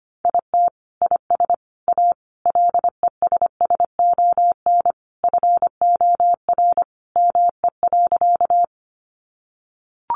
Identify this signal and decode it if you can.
Morse code, heard as a keyed tone, about 25 wpm, 700 Hz. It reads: IT IS A LESSON FOR ME.
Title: Sentences from Top 1200 Words 25wpm